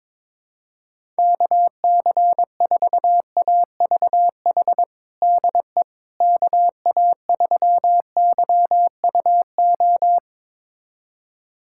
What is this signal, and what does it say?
Morse code, heard as a keyed tone, about 22 wpm, 700 Hz. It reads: KC4AVH DE KA3YUO
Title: USA Callsign DE Callsign - 6 Characters - 22wpm